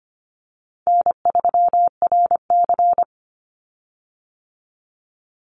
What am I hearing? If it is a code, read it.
Morse code, heard as a keyed tone, about 25 wpm, 700 Hz. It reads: N3RC
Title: Easy Call Signs Repeated 2x - 25wpm